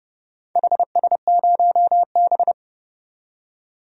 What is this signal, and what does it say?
Morse code, heard as a keyed tone, about 30 wpm, 700 Hz. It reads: HS0B
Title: DXCC-100 Call Signs - 4 Characters - 30wpm